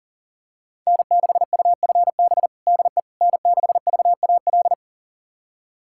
Morse code, heard as a keyed tone, about 40 wpm, 700 Hz. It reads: N6UFB DE N6VAL
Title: USA Callsign DE Callsign - 5 Characters - 40wpm